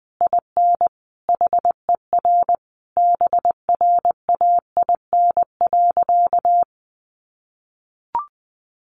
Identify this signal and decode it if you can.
Morse code, heard as a keyed tone, about 20 wpm, 700 Hz. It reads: IN HER BRAIN.
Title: Sentences from Top 1100 Words 20wpm